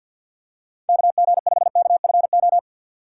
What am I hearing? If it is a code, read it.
Morse code, heard as a keyed tone, about 51 wpm, 700 Hz. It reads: KG5KFO
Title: USA Call Signs - 3x Character Spacing - 50wpm